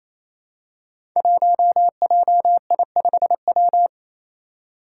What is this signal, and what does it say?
Morse code, heard as a keyed tone, about 27 wpm, 700 Hz. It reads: 1JI5W
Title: Five Challenging Letter-Number Combinations 28wpm